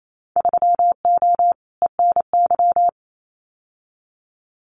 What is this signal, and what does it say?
Morse code, heard as a keyed tone, about 28 wpm, 700 Hz. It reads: 3O ENY
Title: Winter Field Day Exchanges Encore 28wpm